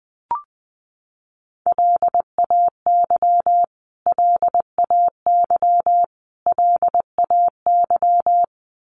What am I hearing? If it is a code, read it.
Morse code, heard as a keyed tone, about 20 wpm, 700 Hz. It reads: LAY LAY LAY
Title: Top 500 Words Repeated 20wpm